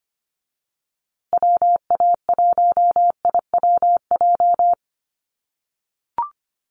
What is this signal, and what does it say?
Morse code, heard as a keyed tone, about 25 wpm, 700 Hz. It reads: WA1IWJ
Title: USA Call Signs Repeated 2x 25wpm